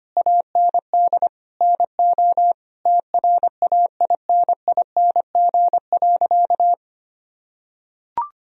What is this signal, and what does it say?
Morse code, heard as a keyed tone, about 25 wpm, 700 Hz. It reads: AND NO TRAINING.
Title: Sentences from Top 1000 Words - Review Part 4 - 25wpm